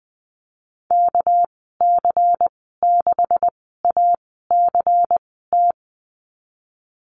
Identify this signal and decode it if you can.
Morse code, heard as a keyed tone, about 20 wpm, spaced as for 14 wpm, 700 Hz. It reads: KC6ACT
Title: 219 Lesson 35g - Number 0 - Call Signs 20wpm